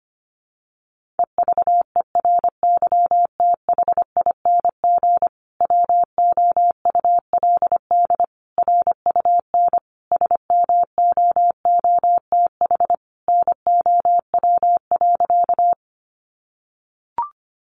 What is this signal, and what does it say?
Morse code, heard as a keyed tone, about 25 wpm, 700 Hz. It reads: EVERYTHING WOULD RUN SMOOTH NOW.
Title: Sentences from Top 1400 Words 25wpm